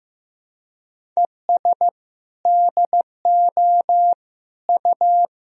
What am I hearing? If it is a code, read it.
Morse code, heard as a keyed tone, about 15 wpm, 700 Hz. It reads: ES DO U
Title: Sentences from Top 100 Words and Abbreviations 15wpm